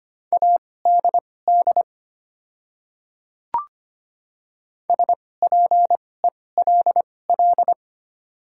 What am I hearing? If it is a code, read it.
Morse code, heard as a keyed tone, about 25 wpm, spaced as for 18 wpm, 700 Hz. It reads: ADD SPELL
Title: 2x Character Spacing - Top 200 Words - 25wpm